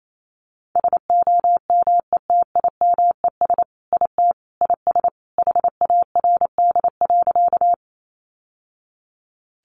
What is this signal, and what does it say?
Morse code, heard as a keyed tone, about 28 wpm, 700 Hz. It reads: SOMETIMES IT IS HARD.
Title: Sentences from Top 600 Words 28wpm